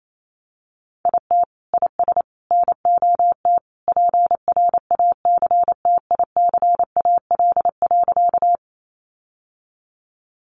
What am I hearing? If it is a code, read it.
Morse code, heard as a keyed tone, about 28 wpm, 700 Hz. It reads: IT IS NOT PRACTICAL.
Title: Sentences from Top 1400 Words 28wpm